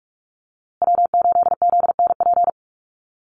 Morse code, heard as a keyed tone, about 44 wpm, 700 Hz. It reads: W8ZNP
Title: USA Call Signs Repeated 3x 45wpm